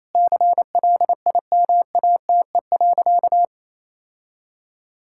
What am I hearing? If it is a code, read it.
Morse code, heard as a keyed tone, about 28 wpm, 700 Hz. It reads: CLIMATE.
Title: Sentences from Top 1600 Words 28wpm